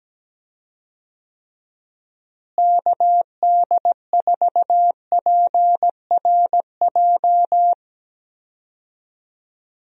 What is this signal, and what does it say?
Morse code, heard as a keyed tone, about 17 wpm, 700 Hz. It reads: KD4PRJ
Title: USA Call Signs - 6 Characters - Repeated 2x - 17wpm